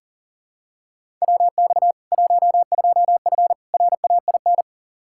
Morse code, heard as a keyed tone, about 40 wpm, 700 Hz. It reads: WX 12F RAIN
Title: K1USN SST QSO Practice 40wpm